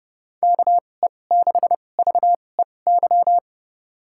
Morse code, heard as a keyed tone, about 31 wpm, spaced as for 22 wpm, 700 Hz. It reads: KE6VEY
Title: USA Call Signs - 2x Character Spacing - 30wpm